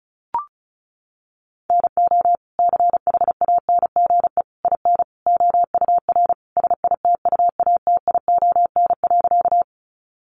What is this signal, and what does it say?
Morse code, heard as a keyed tone, about 34 wpm, 700 Hz. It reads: NO CHANGE IN OUR SITUATION.
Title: Sentences from Top 1200 Words 35wpm